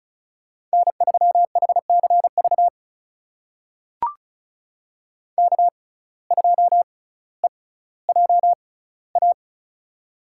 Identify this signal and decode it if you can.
Morse code, heard as a keyed tone, about 36 wpm, spaced as for 18 wpm, 700 Hz. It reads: N3HCV K2EJA
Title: USA Call Signs - 5 Characters - 6x Character Spacing - 35wpm